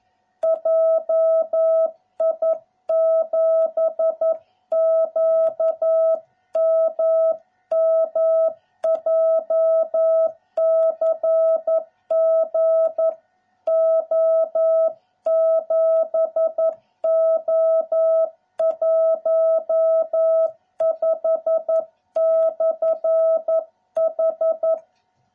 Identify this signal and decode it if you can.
Morse code, heard as a keyed tone, about 11 wpm, 660 Hz. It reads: JI7QMM JCG O7O15/H